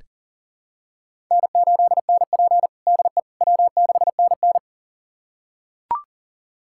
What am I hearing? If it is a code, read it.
Morse code, heard as a keyed tone, about 40 wpm, 700 Hz. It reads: N8NP DE W6NN
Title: USA Callsign DE Callsign - 4 Characters - 40wpm